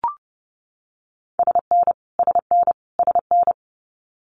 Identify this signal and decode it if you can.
Morse code, heard as a keyed tone, about 30 wpm, 700 Hz. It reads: SN SN SN
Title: Multi-letter Phonemes Repeated 3x 30wpm